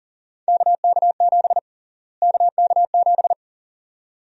Morse code, heard as a keyed tone, about 37 wpm, 700 Hz. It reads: KK7 KK7
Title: USA Call Sign Prefixes - Repeated 2x - 40wpm